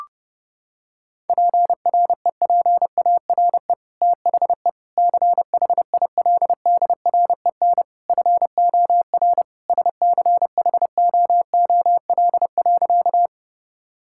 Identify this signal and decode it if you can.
Morse code, heard as a keyed tone, about 30 wpm, 700 Hz. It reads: PREPARE THE CHILDREN FOR SCHOOL.
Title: Sentences from Top 1000 Words - Review Part 6 - 30wpm